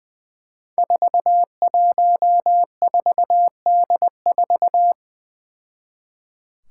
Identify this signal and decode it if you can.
Morse code, heard as a keyed tone, about 20 wpm, 700 Hz. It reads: 414D4
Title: Five Challenging Letter-Number Combinations 20wpm